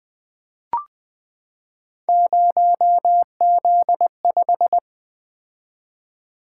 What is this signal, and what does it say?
Morse code, heard as a keyed tone, about 20 wpm, 700 Hz. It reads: 0Z5
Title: Three Letter-Number Combinations - 20wpm